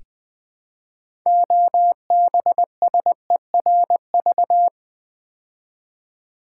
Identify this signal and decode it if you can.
Morse code, heard as a keyed tone, about 20 wpm, 700 Hz. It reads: OBSERV
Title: Build A Word - 2 Letters At A Time - 8-Letter Words - 20wpm